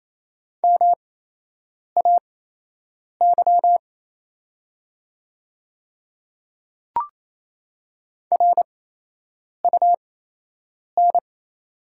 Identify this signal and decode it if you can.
Morse code, heard as a keyed tone, about 28 wpm, spaced as for 8 wpm, 700 Hz. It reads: MAY RUN
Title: 8x Character Spacing - Top 300 Words - 28wpm